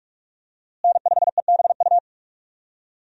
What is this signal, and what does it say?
Morse code, heard as a keyed tone, about 46 wpm, 700 Hz. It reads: N5EBU
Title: USA Call Signs - 5 Characters - Repeated 3x - 45wpm